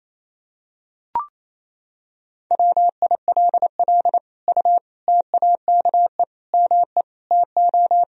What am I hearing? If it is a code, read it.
Morse code, heard as a keyed tone, about 28 wpm, 700 Hz. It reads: WILL U TAKE ME TO
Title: Sentences from Top 200 Words and Abbreviations 28wpm